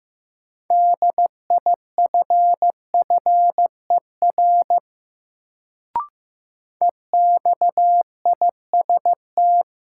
Morse code, heard as a keyed tone, about 15 wpm, 700 Hz. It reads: DIFFER EXIST
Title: Sentences from Top 1100 Words - New Word Drill Alt - 15wpm